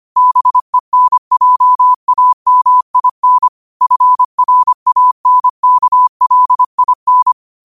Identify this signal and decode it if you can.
Morse code, heard as a keyed tone, about 25 wpm, 1000 Hz. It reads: BENJAMIN FRANKLIN